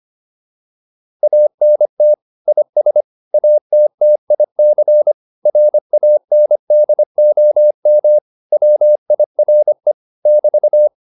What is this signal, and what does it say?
Morse code, heard as a keyed tone, about 25 wpm, 600 Hz. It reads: ANT IS ATTIC RANDOM WIRE <BT>